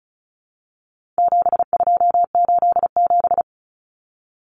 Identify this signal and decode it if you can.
Morse code, heard as a keyed tone, about 34 wpm, 700 Hz. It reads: 7287
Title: Common Cut Numbers - 4-Digit 35wpm